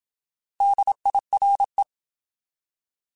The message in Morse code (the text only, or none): DIRE